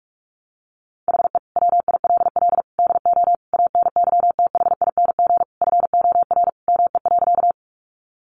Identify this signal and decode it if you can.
Morse code, heard as a keyed tone, about 44 wpm, 700 Hz. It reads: HE WILL DO ANYTHING FOR ME.
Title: Sentences from Top 1000 Words 45wpm